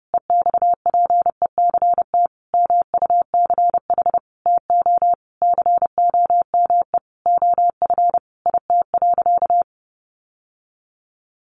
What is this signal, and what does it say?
Morse code, heard as a keyed tone, about 30 wpm, 700 Hz. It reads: EXPECT MUCH TO COME OF IT.
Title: Sentences from Top 1100 Words 30wpm